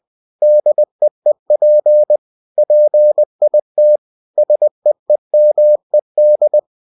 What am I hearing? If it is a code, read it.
Morse code, heard as a keyed tone, about 20 wpm, 600 Hz. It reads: DEEP PIT SEEMED
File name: cw-books / winnie-the-pooh / winnie-20wpm-page-0040